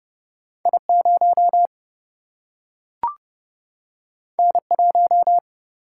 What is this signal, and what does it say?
Morse code, heard as a keyed tone, about 29 wpm, 700 Hz. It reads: I 0 N 1
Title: Single-Letter Single-Number 30wpm